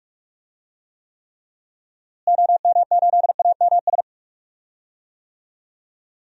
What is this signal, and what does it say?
Morse code, heard as a keyed tone, about 46 wpm, 700 Hz. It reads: OM8AMS
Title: DXCC-100 Call Signs - 6 Characters - Repeated 3x - 45wpm